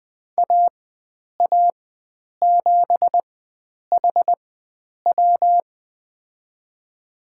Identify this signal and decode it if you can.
Morse code, heard as a keyed tone, about 21 wpm, spaced as for 9 wpm, 700 Hz. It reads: AA7HW
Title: USA Call Signs - 5 Characters - 4x Character Spacing - 20wpm